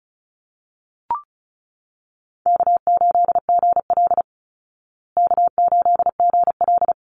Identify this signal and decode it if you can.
Morse code, heard as a keyed tone, about 34 wpm, 700 Hz. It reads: K8GL K8GL
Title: Easy Call Signs Repeated 3x - 35wpm